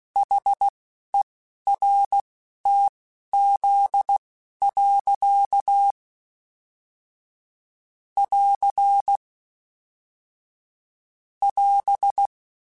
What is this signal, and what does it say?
Morse code, heard as a keyed tone, about 16 wpm, spaced as for 11 wpm, 800 Hz. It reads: HERTZ. <AR> <AS>